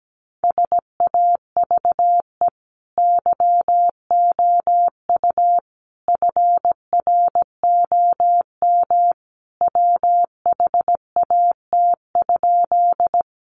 Morse code, heard as a keyed tone, about 17 wpm, 700 Hz. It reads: SAVE YOU FROM WHAT?
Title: Sentences from Top 1100 Words 17wpm